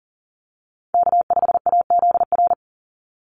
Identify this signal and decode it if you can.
Morse code, heard as a keyed tone, about 40 wpm, 700 Hz. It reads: K5AZR
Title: Call Signs - 40wpm